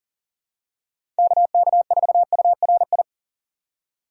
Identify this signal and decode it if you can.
Morse code, heard as a keyed tone, about 39 wpm, 700 Hz. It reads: KK4URI
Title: USA Call Signs - 4x Character Spacing - 40wpm